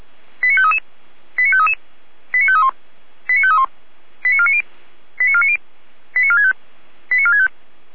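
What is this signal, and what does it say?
Five-tone selective call (ZVEI), heard as quick runs of five tones, tones of about 70 ms. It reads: 89533 89533 89531 89531 89490 89490 89465 89465
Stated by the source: ZVEI five-tone selective call 89518, received by SDR.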